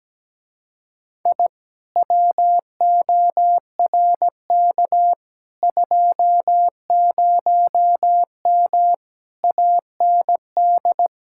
Morse code, heard as a keyed tone, about 17 wpm, 700 Hz. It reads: I WORK 20M AND
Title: Speed-Racing Common QSO Phrases - 17wpm